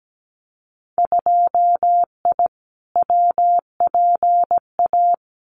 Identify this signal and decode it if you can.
Morse code, heard as a keyed tone, about 17 wpm, 700 Hz. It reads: 2I WPA
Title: Winter Field Day Exchanges 17wpm